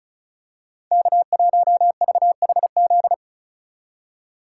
Morse code, heard as a keyed tone, about 34 wpm, 700 Hz. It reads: K1VHZ
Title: USA Call Signs - 5 Characters - 4x Character Spacing - 35wpm